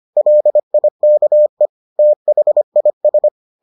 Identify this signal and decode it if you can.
Morse code, heard as a keyed tone, about 25 wpm, 600 Hz. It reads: LIKE THIS